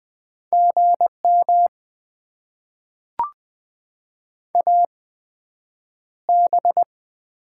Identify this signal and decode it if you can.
Morse code, heard as a keyed tone, about 20 wpm, spaced as for 9 wpm, 700 Hz. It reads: GM AB
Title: Top 90 QSO Elements - 8x Character Spacing - 20wpm